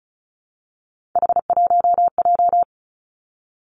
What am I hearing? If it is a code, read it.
Morse code, heard as a keyed tone, about 36 wpm, 700 Hz. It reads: H1J
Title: Three Challenging Letter-Number Combinations 35wpm